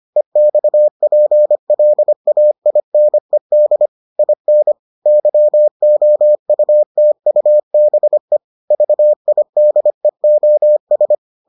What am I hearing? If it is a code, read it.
Morse code, heard as a keyed tone, about 25 wpm, 600 Hz. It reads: EXPLAINED IN YOUTUBE VIDEOS